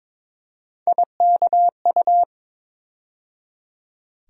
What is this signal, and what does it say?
Morse code, heard as a keyed tone, about 21 wpm, 700 Hz. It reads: IKU
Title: USA Call Sign Postfixes - 3 Letters 22wpm